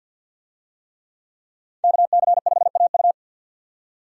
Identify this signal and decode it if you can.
Morse code, heard as a keyed tone, about 47 wpm, 700 Hz. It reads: KC5AU
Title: USA Call Signs - 5 Characters - 50wpm